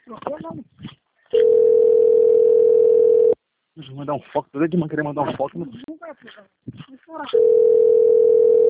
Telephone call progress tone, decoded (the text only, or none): ringback tone